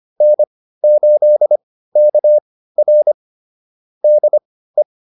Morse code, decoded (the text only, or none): N8KR DE